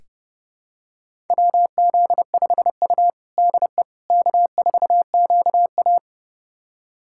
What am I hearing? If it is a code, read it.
Morse code, heard as a keyed tone, about 30 wpm, 700 Hz. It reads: WZ5U DE K4QA
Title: USA Callsign DE Callsign - 4 Characters - 30wpm